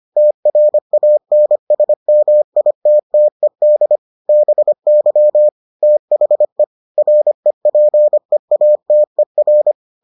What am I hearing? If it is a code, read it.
Morse code, heard as a keyed tone, about 25 wpm, 600 Hz. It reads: TRANSMITTED BY THE REPEATER